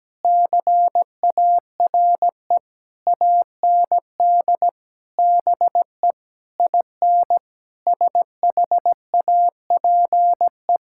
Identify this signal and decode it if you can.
Morse code, heard as a keyed tone, about 17 wpm, 700 Hz. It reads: CARE AND BE IN SHAPE
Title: Sentences from Top 1000 Words - Review Part 6 - 17wpm